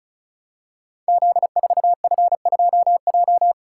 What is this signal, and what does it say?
Morse code, heard as a keyed tone, about 36 wpm, 700 Hz. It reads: Z4F2J
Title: Code Groups - 5 Letter:Numbers - 35wpm